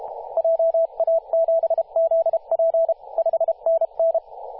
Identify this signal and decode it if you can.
Morse code, heard as a keyed tone, about 32 wpm, 660 Hz, with radio noise. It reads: JA7ZP 5NN
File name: posts / 2016-oct-05-2124z-t31t-18-072mhz-cw